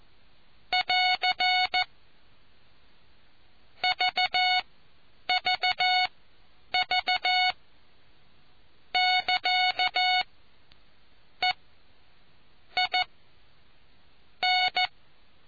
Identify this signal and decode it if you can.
Morse code, heard as a keyed tone, about 14 wpm, spaced as for 7 wpm, 730 Hz. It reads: <AR> VVV <KA> EIN